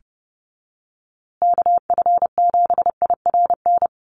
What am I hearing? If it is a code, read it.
Morse code, heard as a keyed tone, about 30 wpm, 700 Hz. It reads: KF7IRN
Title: USA Call Signs Repeated 2x 30wpm